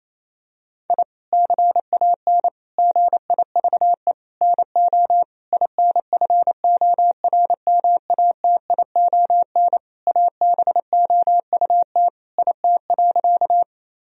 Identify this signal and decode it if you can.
Morse code, heard as a keyed tone, about 28 wpm, 700 Hz. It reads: I CAN GIVE NO INFORMATION ABOUT IT.
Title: Sentences from Top 1000 Words - Review Part 3 - 28wpm